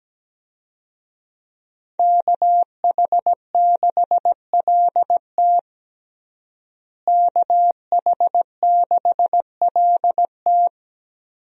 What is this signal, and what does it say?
Morse code, heard as a keyed tone, about 17 wpm, 700 Hz. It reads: KH6LT KH6LT
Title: USA Call Signs - 5 Characters - Repeated 2x - 17wpm